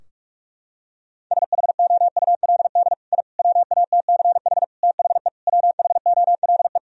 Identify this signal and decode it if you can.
Morse code, heard as a keyed tone, about 46 wpm, 700 Hz. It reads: SHOULD I WATCH THE WHOLE
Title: Sentences from Top 1000 Words - Review Part 4 - 45wpm